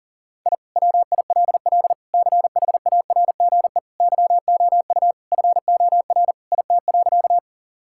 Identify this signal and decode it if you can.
Morse code, heard as a keyed tone, about 37 wpm, 700 Hz. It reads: I WILL CHARGE YOU FOR IT.
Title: Sentences from Top 1000 Words - Review Part 6 - 40wpm